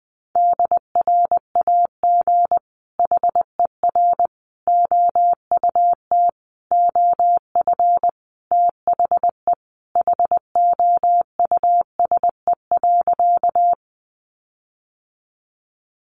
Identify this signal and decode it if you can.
Morse code, heard as a keyed tone, about 20 wpm, 700 Hz. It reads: DRAG HER OUT OF THE HOUSE.
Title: Sentences from Top 1500 Words 20wpm